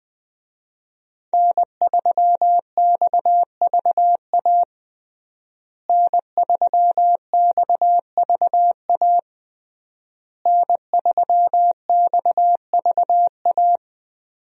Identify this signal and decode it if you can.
Morse code, heard as a keyed tone, about 21 wpm, 700 Hz. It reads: N3XVA N3XVA N3XVA
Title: USA Call Signs - 5 Characters - Repeated 3x - 20wpm